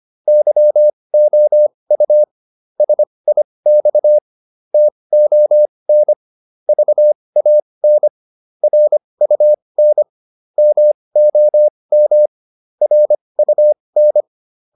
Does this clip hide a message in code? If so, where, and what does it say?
Yes: Morse code from 0.3 to 14.2 s, heard as a keyed tone, about 25 wpm, spaced as for 20 wpm, 600 Hz. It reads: YOU SIX TON VAN RUN MOM RUN